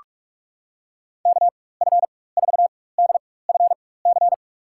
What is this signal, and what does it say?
Morse code, heard as a keyed tone, about 46 wpm, spaced as for 21 wpm, 700 Hz. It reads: KF4DFC
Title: USA Call Signs - 6 Characters - 4x Character Spacing - 45wpm